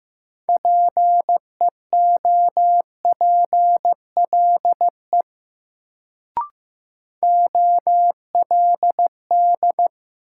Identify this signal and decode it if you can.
Morse code, heard as a keyed tone, about 15 wpm, 700 Hz. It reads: PEOPLE OLD
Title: Sentences from Top 200 Words - New Word Drill - 15wpm